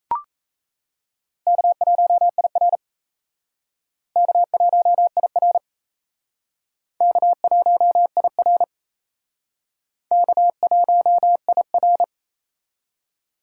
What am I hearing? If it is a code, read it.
Morse code, heard as a keyed tone, about 35 wpm, 700 Hz. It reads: K1IR K1IR K1IR K1IR
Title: Speed Racing - USA 1x2 Call Signs 28wpm